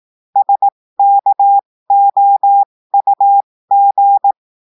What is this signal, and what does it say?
Morse code, heard as a keyed tone, about 18 wpm, spaced as for 15 wpm, 800 Hz. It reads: SKOUG
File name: cours-du-lundi files / cours-1580-texte-chiffre-15-mpm